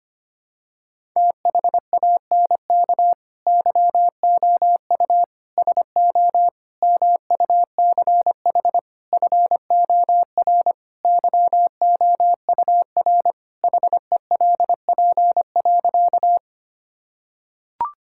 Morse code, heard as a keyed tone, about 25 wpm, 700 Hz. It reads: THANK YOU SO MUCH FOR YOUR HELP.
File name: practice / 25wpm / Sentences with 4-Word Phrases - 25wpm